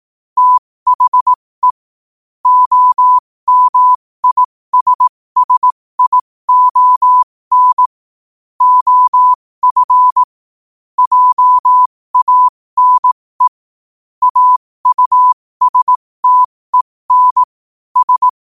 Morse code, heard as a keyed tone, about 18 wpm, spaced as for 15 wpm, 1000 Hz. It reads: THE OMISSION OF JANE AUSTEN S
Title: 15 WPM morse code quotes for Thu, 29 Feb 2024